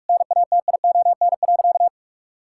Morse code, heard as a keyed tone, about 41 wpm, 700 Hz. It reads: NATION.
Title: Sentences from Top 1000 Words - Review Part 6 - 45wpm